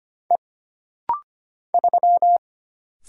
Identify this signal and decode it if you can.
Morse code, heard as a keyed tone, about 25 wpm, 700 Hz. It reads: E 3
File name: Single Character (Letter:Number:Punctuation:Prosign) - Rapid-Fire - 25wpm